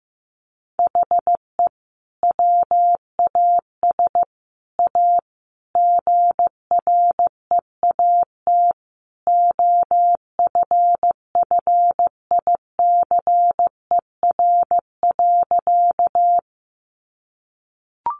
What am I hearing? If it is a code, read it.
Morse code, heard as a keyed tone, about 15 wpm, 700 Hz. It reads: HE WAS A GREAT OFFICER.
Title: Sentences from Top 1200 Words 15wpm